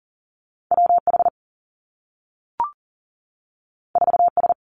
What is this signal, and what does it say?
Morse code, heard as a keyed tone, about 40 wpm, 700 Hz. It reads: W H 4 S
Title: Two Challenging Letter-Number Combinations 40wpm